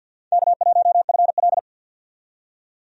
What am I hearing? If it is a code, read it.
Morse code, heard as a keyed tone, about 50 wpm, 700 Hz. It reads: K1FL
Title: USA Call Signs - 4 Characters - 50wpm